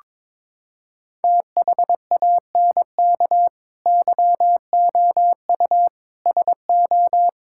Morse code, heard as a keyed tone, about 22 wpm, 700 Hz. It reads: THANK YOU SO
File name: Sets of 9 Words - 22wpm